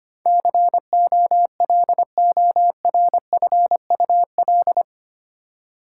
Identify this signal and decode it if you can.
Morse code, heard as a keyed tone, about 25 wpm, 700 Hz. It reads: COLORFUL
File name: Eight Letter Words - 25wpm